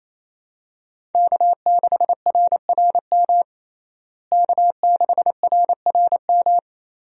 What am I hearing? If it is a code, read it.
Morse code, heard as a keyed tone, about 28 wpm, 700 Hz. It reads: K6RRM K6RRM
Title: USA Call Signs - 5 Characters - Repeated 2x - 28wpm